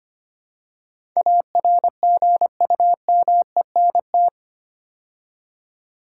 Morse code, heard as a keyed tone, about 25 wpm, 700 Hz. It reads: ARGUMENT
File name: 8 Letter Words - First 3 Letters Given - 25wpm